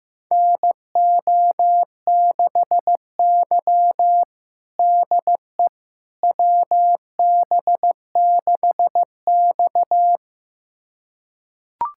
Morse code, heard as a keyed tone, about 15 wpm, 700 Hz. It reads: NO6Y DE WB6X
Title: USA Callsign DE Callsign - 4 Characters - 15wpm